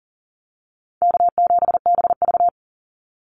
Morse code, heard as a keyed tone, about 40 wpm, 700 Hz. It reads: K7BV
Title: Call Signs - 40wpm